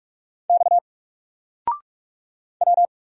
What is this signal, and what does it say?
Morse code, heard as a keyed tone, about 44 wpm, 700 Hz. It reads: X W
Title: Single Character (Letter:Number:Punctuation:Prosign) Alt 45wpm